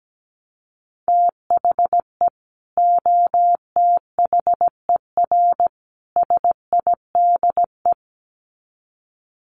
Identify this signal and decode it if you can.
Morse code, heard as a keyed tone, about 17 wpm, 700 Hz. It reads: THE OTHER SIDE
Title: Sets of 3 Words 17wpm